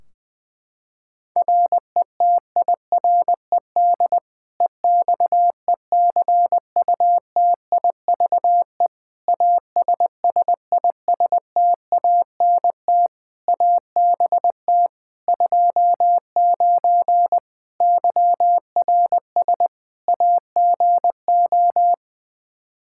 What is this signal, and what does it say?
Morse code, heard as a keyed tone, about 21 wpm, 700 Hz. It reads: RETIRED EXECUTIVE ASSISTANT ABT 29 YRS AGO
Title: Practice QSO - Exchange 3 - More Info - 20wpm